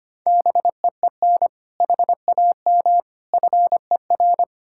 Morse code, heard as a keyed tone, about 33 wpm, 700 Hz. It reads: BEEN HAM FER